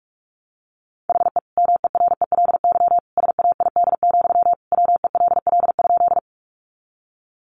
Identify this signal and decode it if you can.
Morse code, heard as a keyed tone, about 44 wpm, 700 Hz. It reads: HE MERELY SAID, WELL?
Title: Sentences from Top 1400 Words 45wpm